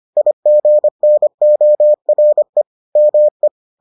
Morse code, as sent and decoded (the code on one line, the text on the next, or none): .. --. -. --- .-. / -- .
IGNORE ME